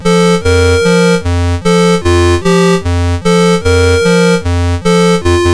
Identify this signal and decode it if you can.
There is no code.